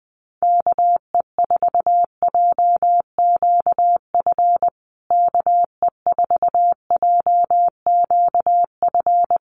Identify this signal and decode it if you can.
Morse code, heard as a keyed tone, about 20 wpm, 700 Hz. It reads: KE4JQF KE4JQF
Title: USA Call Signs Repeated 3x 20wpm